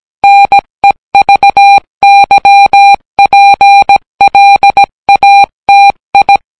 Morse code, heard as a keyed tone, about 17 wpm, 800 Hz. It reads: NEVYPLATI